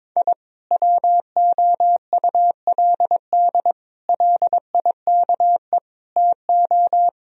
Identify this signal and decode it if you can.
Morse code, heard as a keyed tone, about 23 wpm, 700 Hz. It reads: I WOULD LIKE TO
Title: Speed-Racing - Sets of 5 Words 22wpm